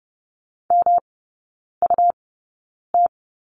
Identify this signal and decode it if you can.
Morse code, heard as a keyed tone, about 30 wpm, spaced as for 9 wpm, 700 Hz. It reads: MUT
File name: Lesson 18e - Letter m - Three Characters 30wpm